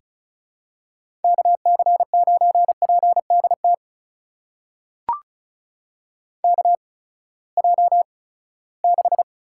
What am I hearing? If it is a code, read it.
Morse code, heard as a keyed tone, about 35 wpm, spaced as for 20 wpm, 700 Hz. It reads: KC9PDT KJ6
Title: USA Call Signs - 6 Characters - 8x Character Spacing - 35wpm